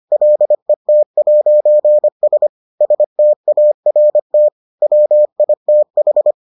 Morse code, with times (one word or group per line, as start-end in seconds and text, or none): 0.1-2.5 LET'S
2.8-4.5 START
4.8-6.3 WITH